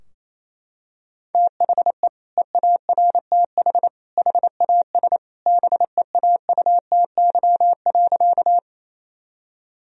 Morse code, heard as a keyed tone, about 28 wpm, 700 Hz. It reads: THE EARTH HAS BEAUTY.
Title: Sentences from Top 1000 Words - Review Part 2 - 28wpm